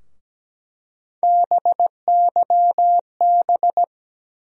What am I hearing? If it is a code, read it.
Morse code, heard as a keyed tone, about 17 wpm, 700 Hz. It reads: BYB
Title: USA Call Sign Postfixes - 3 Letters 17wpm